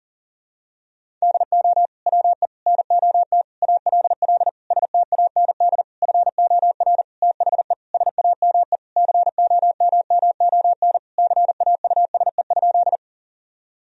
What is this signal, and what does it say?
Morse code, heard as a keyed tone, about 40 wpm, 700 Hz. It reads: DO WE NOT ALL STAND FOR THE SAME COMMON CAUSE?
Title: Sentences from Top 1000 Words - Review Part 3 - 40wpm